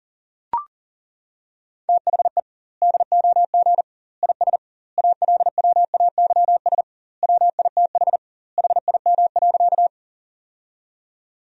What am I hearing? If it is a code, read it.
Morse code, heard as a keyed tone, about 40 wpm, 700 Hz. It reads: THE DOG IS ALWAYS WITH HIM.
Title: 2x Word-Spacing - Sentences from Top 400 Words - 40wpm